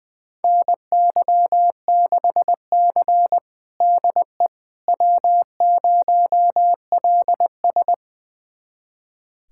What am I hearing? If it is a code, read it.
Morse code, heard as a keyed tone, about 21 wpm, 700 Hz. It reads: NY6C DE W0LS
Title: USA Callsign DE Callsign - 4 Characters - 20wpm